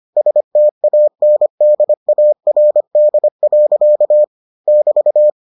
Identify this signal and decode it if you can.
Morse code, heard as a keyed tone, about 25 wpm, 600 Hz. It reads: STANDARD. <BT>